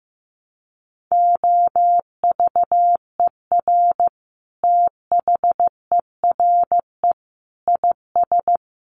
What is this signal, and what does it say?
Morse code, heard as a keyed tone, about 15 wpm, 700 Hz. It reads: OVER THERE IS
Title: Sentences from Top 1200 Words 15wpm